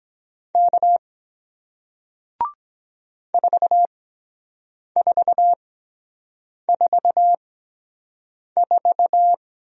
Morse code, heard as a keyed tone, about 23 wpm, spaced as for 7 wpm, 700 Hz. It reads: K 4444